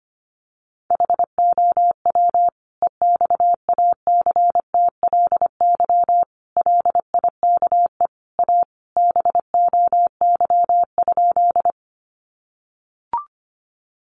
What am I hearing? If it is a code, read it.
Morse code, heard as a keyed tone, about 25 wpm, 700 Hz. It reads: HOW EXACTLY LIKE A BOY?
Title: Sentences from Top 900 Words 25wpm